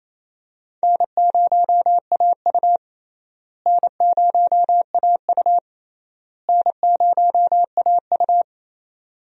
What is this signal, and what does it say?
Morse code, heard as a keyed tone, about 28 wpm, 700 Hz. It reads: N0AU N0AU N0AU
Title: USA Call Signs - 4 Characters - Repeated 3x - 28wpm